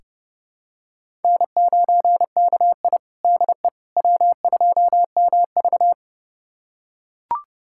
Morse code, heard as a keyed tone, about 30 wpm, 700 Hz. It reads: N9KI DE W2MV
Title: USA Callsign DE Callsign - 4 Characters - 30wpm